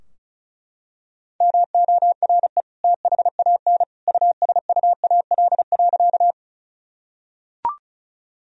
Morse code, heard as a keyed tone, about 36 wpm, 700 Hz. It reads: MORE THAN USUAL.
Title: Sentences from Top 1000 Words - Review Part 2 - 35wpm